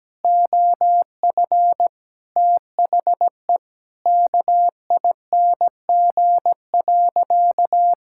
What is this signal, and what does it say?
Morse code, heard as a keyed tone, about 17 wpm, 700 Hz. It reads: OF THE KING.
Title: Sentences from Top 1000 Words - Review Part 6 - 17wpm